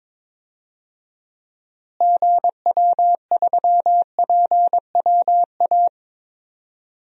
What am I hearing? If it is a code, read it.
Morse code, heard as a keyed tone, about 22 wpm, 700 Hz. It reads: GW3PWA
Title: DXCC-100 Call Signs - 6 Characters - Repeated 3x - 22wpm